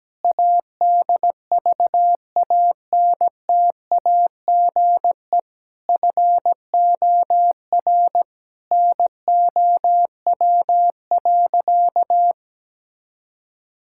Morse code, heard as a keyed tone, about 17 wpm, 700 Hz. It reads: ADVANTAGE FOR NOW.
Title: Sentences from Top 1000 Words - Review Part 6 - 17wpm